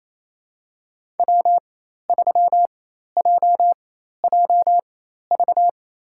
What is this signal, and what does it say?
Morse code, heard as a keyed tone, about 27 wpm, spaced as for 13 wpm, 700 Hz. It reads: W3JJV
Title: USA Call Signs - 5 Characters - 4x Character Spacing - 28wpm